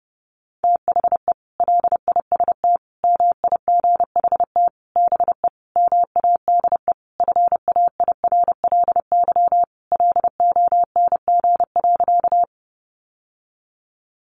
Morse code, heard as a keyed tone, about 30 wpm, 700 Hz. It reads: THE LIST MIGHT BE MADE FAIRLY LONG.
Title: Sentences from Top 1100 Words 30wpm